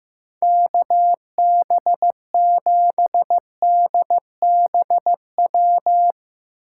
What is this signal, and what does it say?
Morse code, heard as a keyed tone, about 15 wpm, 700 Hz. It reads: KB7DBW
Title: USA Call Signs - 2x Character Spacing - 15wpm